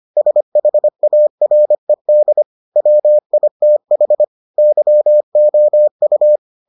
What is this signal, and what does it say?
Morse code, heard as a keyed tone, about 25 wpm, 600 Hz. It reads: SHARED WITH YOU